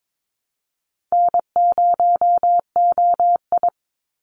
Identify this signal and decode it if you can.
Morse code, heard as a keyed tone, about 22 wpm, 700 Hz. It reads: N0OI
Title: NA SOTA Activator Callsigns 22wpm